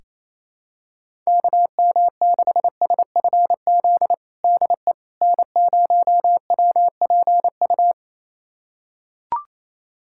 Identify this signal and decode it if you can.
Morse code, heard as a keyed tone, about 28 wpm, 700 Hz. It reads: KM6SFZ DE N0WPU
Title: USA Callsign DE Callsign 28wpm